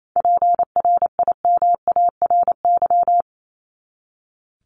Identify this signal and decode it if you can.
Morse code, heard as a keyed tone, about 28 wpm, 700 Hz. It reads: PRIMARY
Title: LY Suffix - Top 100 Words with Base Words 28wpm